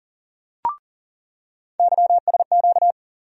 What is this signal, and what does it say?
Morse code, heard as a keyed tone, about 40 wpm, 700 Hz. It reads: YSQ
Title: USA Call Sign Postfixes - 3 Letters 40wpm